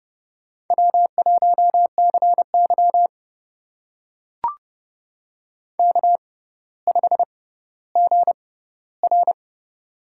Morse code, heard as a keyed tone, about 31 wpm, spaced as for 15 wpm, 700 Hz. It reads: W1CY K5GR
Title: USA Call Signs - 6x Character Spacing - 30wpm